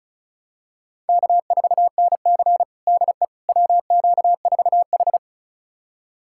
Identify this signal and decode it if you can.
Morse code, heard as a keyed tone, about 36 wpm, 700 Hz. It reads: K4NC DE WQ4H
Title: USA Callsign DE Callsign - 4 Characters - 35wpm